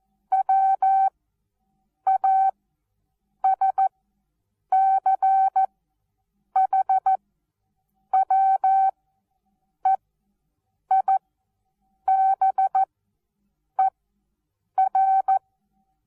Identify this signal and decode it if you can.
Morse code, heard as a keyed tone, about 14 wpm, spaced as for 7 wpm, 770 Hz. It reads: WASCHWEIBER
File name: Onlinelehrgang / morsen / DJ4UF Morsen19